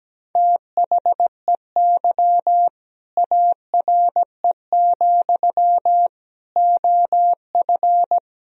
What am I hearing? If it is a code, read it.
Morse code, heard as a keyed tone, about 17 wpm, 700 Hz. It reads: THEY ARE, OF
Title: Sentences from Top 1000 Words - Review Part 4 - 17wpm